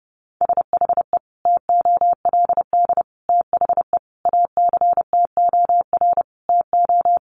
Morse code, heard as a keyed tone, about 30 wpm, 700 Hz. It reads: SHE TOLD THE ACTOR TO
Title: Sentences from Top 1500 Words 30wpm